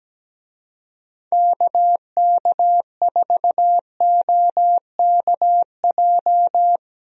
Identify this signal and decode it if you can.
Morse code, heard as a keyed tone, about 17 wpm, 700 Hz. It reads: KK4OKJ
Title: USA Call Signs - 6 Characters - Repeated 2x - 17wpm